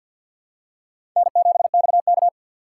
Morse code, heard as a keyed tone, about 53 wpm, 700 Hz. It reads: N7XK
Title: USA Call Signs - 4 Characters - 6x Character Spacing - 50wpm